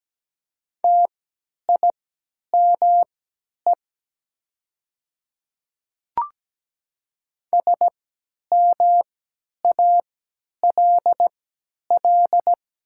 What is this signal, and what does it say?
Morse code, heard as a keyed tone, about 17 wpm, spaced as for 10 wpm, 700 Hz. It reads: TIME SMALL
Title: 3x Character Spacing - Top 200 Words - 17wpm